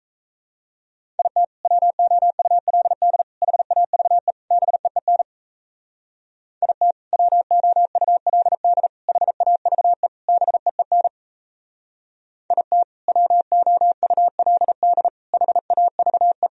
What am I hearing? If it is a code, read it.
Morse code, heard as a keyed tone, about 37 wpm, 700 Hz. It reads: IT WOULD HAVE BEEN IT WOULD HAVE BEEN IT WOULD HAVE